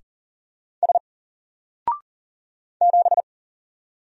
Morse code, heard as a keyed tone, about 40 wpm, 700 Hz. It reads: S 7
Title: Single Letter-Number 40wpm